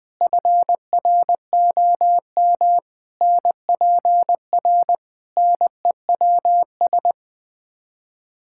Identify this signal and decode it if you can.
Morse code, heard as a keyed tone, about 20 wpm, 700 Hz. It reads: FROM NPR NEWS
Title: Build A Phrase - 5-Word Phrases - 20wpm